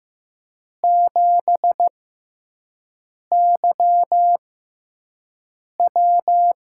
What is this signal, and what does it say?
Morse code, heard as a keyed tone, about 15 wpm, spaced as for 5 wpm, 700 Hz. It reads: 7YW
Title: USA Call Signs - 6x Character Spacing - 15wpm